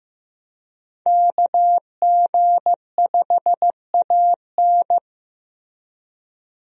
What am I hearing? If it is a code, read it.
Morse code, heard as a keyed tone, about 15 wpm, 700 Hz. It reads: KG5AN
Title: USA Call Signs - 5 Characters - Repeated 2x - 15wpm